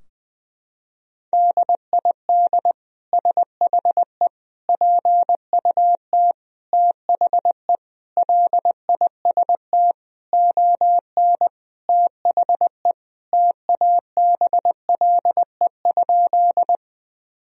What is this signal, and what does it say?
Morse code, heard as a keyed tone, about 20 wpm, 700 Hz. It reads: DID SHE PUT THE LIST ON THE TABLE?
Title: Sentences from Top 1000 Words - Review Part 2 - 20wpm